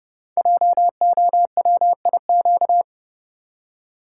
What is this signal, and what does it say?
Morse code, heard as a keyed tone, about 29 wpm, 700 Hz. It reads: JOWIQ
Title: Code Groups - 5 Letters - 30wpm